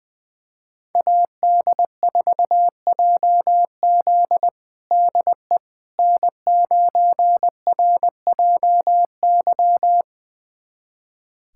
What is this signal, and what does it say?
Morse code, heard as a keyed tone, about 20 wpm, 700 Hz. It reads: AD4JZ DE N9RJY
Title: USA Callsign DE Callsign - 5 Characters - 20wpm